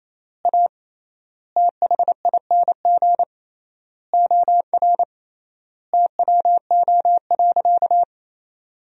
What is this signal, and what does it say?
Morse code, heard as a keyed tone, about 28 wpm, 700 Hz. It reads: A THING OR TWO.
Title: 3x Word-Spacing - Sentences from Top 300 Words - 28wpm